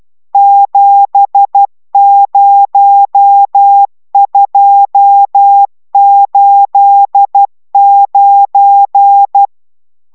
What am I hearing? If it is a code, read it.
Morse code, heard as a keyed tone, about 12 wpm, 800 Hz. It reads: 70289